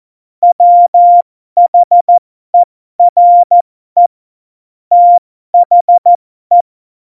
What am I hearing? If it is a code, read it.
Morse code, heard as a keyed tone, about 14 wpm, spaced as for 12 wpm, 700 Hz. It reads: WHERE THE
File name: Morsekurs / Witze